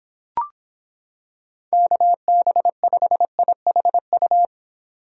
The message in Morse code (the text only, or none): KB5IHU